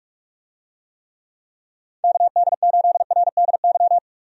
Morse code, heard as a keyed tone, about 43 wpm, 700 Hz. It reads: KD8RDY
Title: USA Call Signs - 6 Characters - 45wpm